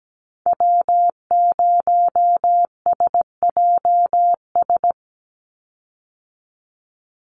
Morse code, heard as a keyed tone, about 17 wpm, 700 Hz. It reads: W0SJS
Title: Call Signs Repeated 2x - 17wpm